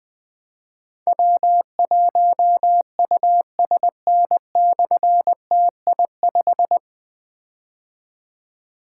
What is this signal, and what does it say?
Morse code, heard as a keyed tone, about 20 wpm, 700 Hz. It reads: W1USN/TI5
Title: Difficult Call Signs 20wpm